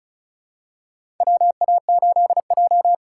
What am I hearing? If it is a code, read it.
Morse code, heard as a keyed tone, about 36 wpm, 700 Hz. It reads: WA8J
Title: USA Call Signs - 3x Character Spacing - 35wpm